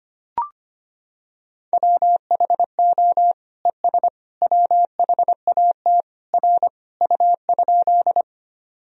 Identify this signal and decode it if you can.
Morse code, heard as a keyed tone, about 25 wpm, 700 Hz. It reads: WHO ES WHAT R U?
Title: Sentences from Top 100 Words and Abbreviations 25wpm